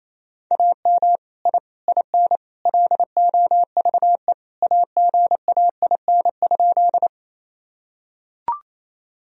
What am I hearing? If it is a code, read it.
Morse code, heard as a keyed tone, about 28 wpm, 700 Hz. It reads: AM I IN LOVE AGAIN?
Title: Sentences from Top 1000 Words - Review Part 2 - 28wpm